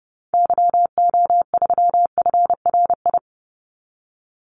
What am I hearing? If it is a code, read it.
Morse code, heard as a keyed tone, about 30 wpm, 700 Hz. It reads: YO3FRI
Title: Call Signs Repeated 3x - 30wpm